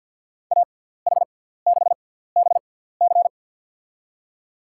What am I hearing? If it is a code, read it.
Morse code, heard as a keyed tone, about 51 wpm, spaced as for 17 wpm, 700 Hz. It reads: AH6BC
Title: USA Call Signs - 5 Characters - 6x Character Spacing - 50wpm